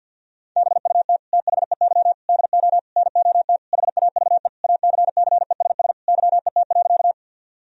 Morse code, heard as a keyed tone, about 50 wpm, 700 Hz. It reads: BUT THEY DO NOT HAVE ACCESS YET.
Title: Sentences from Top 1000 Words - Review Part 4 - 50wpm